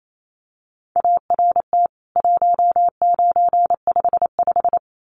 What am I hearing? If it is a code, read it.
Morse code, heard as a keyed tone, about 28 wpm, 700 Hz. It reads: ART 1955
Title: CWT Exchange 28wpm